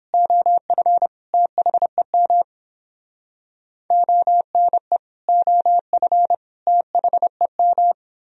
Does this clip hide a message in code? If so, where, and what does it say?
Yes: Morse code from 0.1 to 7.9 s, heard as a keyed tone, about 30 wpm, 700 Hz. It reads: OF THEM ONE OF THEM